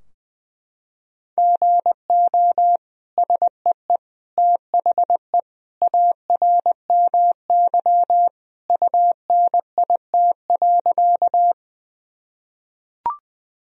Morse code, heard as a keyed tone, about 21 wpm, 700 Hz. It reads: GO SEE THE ARMY UNIT.
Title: Sentences from Top 1000 Words - Review Part 2 - 20wpm